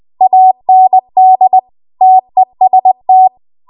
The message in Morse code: AND TEST